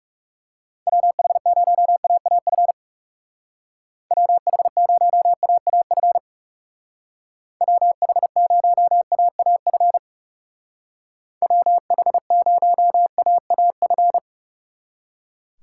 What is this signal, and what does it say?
Morse code, heard as a keyed tone, about 36 wpm, 700 Hz. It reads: WH0AAF WH0AAF WH0AAF WH0AAF
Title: Speed Racing - USA Call Signs 30wpm